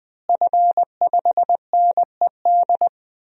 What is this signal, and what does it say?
Morse code, heard as a keyed tone, about 20 wpm, 700 Hz. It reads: F5NED
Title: DXCC-100 Call Signs - 5 Characters - Repeated 3x - 20wpm